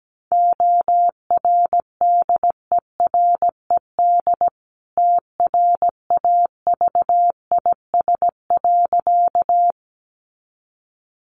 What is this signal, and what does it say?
Morse code, heard as a keyed tone, about 17 wpm, 700 Hz. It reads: ORDERED TRAVIS.
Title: A Sound Of Thunder 17wpm